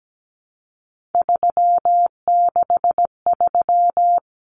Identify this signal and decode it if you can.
Morse code, heard as a keyed tone, about 17 wpm, 700 Hz. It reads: 363
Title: Common Cut Numbers - 3-Digit 17wpm